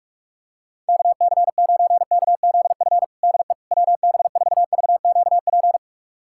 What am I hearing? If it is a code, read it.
Morse code, heard as a keyed tone, about 41 wpm, 700 Hz. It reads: KC9KZF DE WB4VQP
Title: USA Callsign DE Callsign - 6 Characters - 45wpm